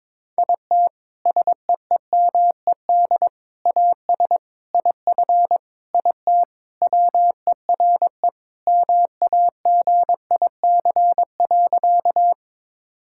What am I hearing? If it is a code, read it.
Morse code, heard as a keyed tone, about 21 wpm, 700 Hz. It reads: IT SEEMED AS IF IT WERE MAGIC.
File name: practice / 22wpm / Sentences with 4-Word Phrases - 22wpm